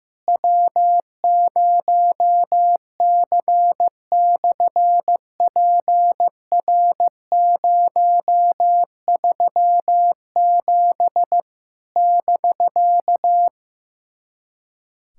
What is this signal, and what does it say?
Morse code, heard as a keyed tone, about 15 wpm, 700 Hz. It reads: W0C/PR037 <BK>
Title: SOTA QSO Practice 15wpm